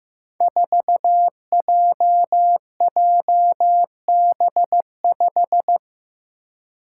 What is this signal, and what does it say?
Morse code, heard as a keyed tone, about 15 wpm, 700 Hz. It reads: 4JJB5
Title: Code Groups - 5 Letter:Numbers - 15wpm